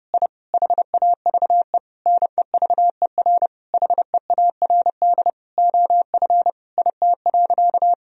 Morse code, heard as a keyed tone, about 30 wpm, 700 Hz. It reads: I HAVE NEVER HEARD OF IT.
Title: Sentences from Top 1000 Words - Review Part 3 - 30wpm